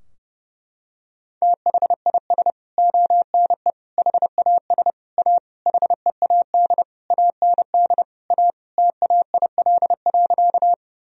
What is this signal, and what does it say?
Morse code, heard as a keyed tone, about 30 wpm, 700 Hz. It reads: THIS ONE HAS A HEAD AND A TAIL.
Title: Sentences from Top 1000 Words - Review Part 3 - 30wpm